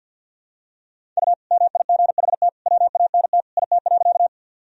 Morse code, heard as a keyed tone, about 51 wpm, 700 Hz. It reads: U MIGHT WANT IT.
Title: Sentences from Top 300 Words and Abbreviations 50wpm